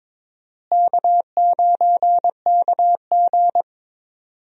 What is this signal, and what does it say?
Morse code, heard as a keyed tone, about 22 wpm, 700 Hz. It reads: K9KG
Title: USA Call Signs - 4 Characters - Repeated 3x - 22wpm